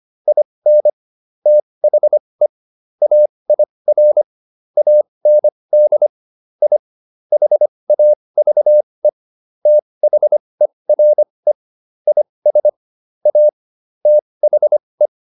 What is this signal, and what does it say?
Morse code, heard as a keyed tone, about 25 wpm, spaced as for 20 wpm, 600 Hz. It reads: IN THE AIR AND I HAVE THERE IS A THE